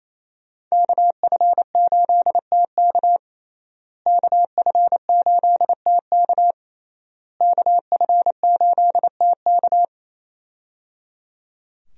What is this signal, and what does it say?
Morse code, heard as a keyed tone, about 28 wpm, 700 Hz. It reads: KF8TK KF8TK KF8TK
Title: USA Call Signs - 5 Characters - Repeated 3x - 28wpm